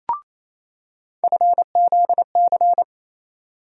Morse code, heard as a keyed tone, about 28 wpm, 700 Hz. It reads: FZC
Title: USA Call Sign Postfixes - 3 Letters 28wpm